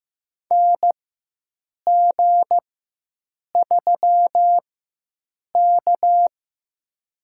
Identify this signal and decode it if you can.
Morse code, heard as a keyed tone, about 15 wpm, spaced as for 7 wpm, 700 Hz. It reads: NG3K
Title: USA Call Signs - 4 Characters - 4x Character Spacing - 15wpm